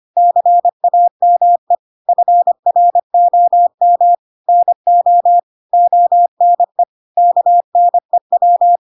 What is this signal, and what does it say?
Morse code, heard as a keyed tone, about 25 wpm, 700 Hz. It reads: CAME FROM NO ONE KNEW
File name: Pinocchio / Pinocchio ch2 25 25 ik1hge